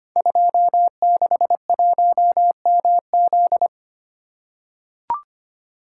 Morse code, heard as a keyed tone, about 25 wpm, 700 Hz. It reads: 261MZ
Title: Code Groups - 5 Letter:Numbers - 25wpm